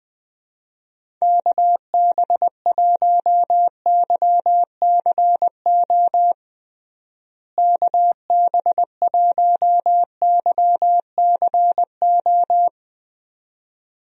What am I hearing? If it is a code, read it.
Morse code, heard as a keyed tone, about 21 wpm, 700 Hz. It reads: KB1YCO KB1YCO
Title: USA Call Signs - 6 Characters - Repeated 2x - 20wpm